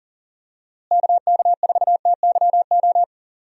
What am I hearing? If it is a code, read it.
Morse code, heard as a keyed tone, about 40 wpm, 700 Hz. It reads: KK4TYO
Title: USA Call Signs - 6 Characters - Repeated 3x - 40wpm